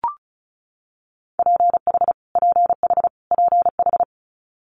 Morse code, heard as a keyed tone, about 36 wpm, 700 Hz. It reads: PH PH PH
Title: Multi-letter Phonemes Repeated 3x 35wpm